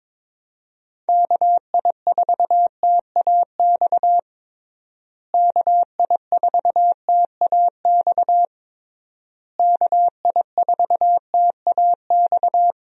Morse code, heard as a keyed tone, about 22 wpm, 700 Hz. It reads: KI4TAX KI4TAX KI4TAX
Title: USA Call Signs - 6 Characters - Repeated 3x - 22wpm